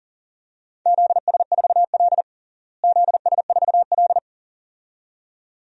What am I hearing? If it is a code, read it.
Morse code, heard as a keyed tone, about 40 wpm, 700 Hz. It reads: ZS4L ZS4L
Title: DXCC-100 Call Signs - 4 Characters - Repeated 2x - 40wpm